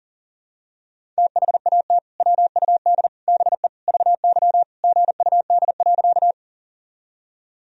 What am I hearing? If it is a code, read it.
Morse code, heard as a keyed tone, about 37 wpm, 700 Hz. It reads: THAT WUD BE VY GUD.
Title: Sentences from Top 200 Words and Abbreviations 40wpm